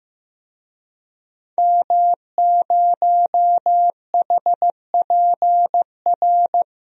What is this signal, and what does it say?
Morse code, heard as a keyed tone, about 15 wpm, 700 Hz. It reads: M0HPR
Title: DXCC-100 Call Signs - 5 Characters - Repeated 2x - 15wpm